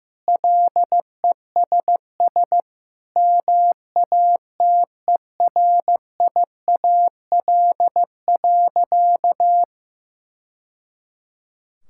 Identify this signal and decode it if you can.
Morse code, heard as a keyed tone, about 16 wpm, 700 Hz. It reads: LESS MATERIAL.
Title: Sentences from Top 1000 Words - Review Part 5 - 15wpm